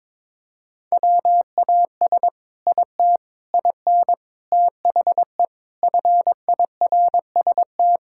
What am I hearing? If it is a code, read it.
Morse code, heard as a keyed tone, about 22 wpm, 700 Hz. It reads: WAS IT IN THE FIRST
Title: Sentences from Top 1000 Words - Review Part 5 - 22wpm